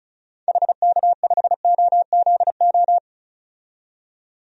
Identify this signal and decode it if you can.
Morse code, heard as a keyed tone, about 36 wpm, 700 Hz. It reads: HK5OZO
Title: DXCC-100 Call Signs - 6 Characters - 35wpm